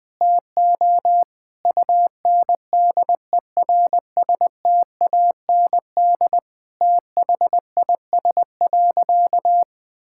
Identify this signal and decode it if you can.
Morse code, heard as a keyed tone, about 20 wpm, 700 Hz. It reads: TO UNDERSTAND THIS.
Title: Sentences with 8-Word Phrases - 20wpm